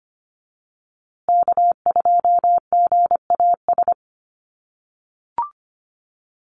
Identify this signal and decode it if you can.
Morse code, heard as a keyed tone, about 25 wpm, 700 Hz. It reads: K2GAS
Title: USA Call Signs Repeated 2x 25wpm